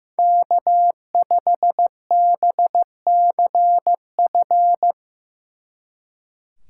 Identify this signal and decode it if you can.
Morse code, heard as a keyed tone, about 17 wpm, 700 Hz. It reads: K5BCF
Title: Speed Racing - USA Call Signs 15wpm